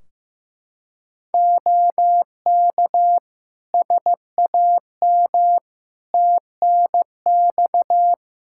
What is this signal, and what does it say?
Morse code, heard as a keyed tone, about 15 wpm, 700 Hz. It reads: OK SAM TNX
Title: Practice QSO - Exchange 4 - Ending - 15wpm